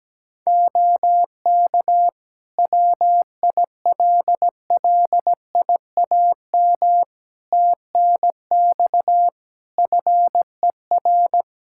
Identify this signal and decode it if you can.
Morse code, heard as a keyed tone, about 17 wpm, 700 Hz. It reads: OK WILLIAM TNX FER
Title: Practice QSO - Exchange 4 - Ending - 17wpm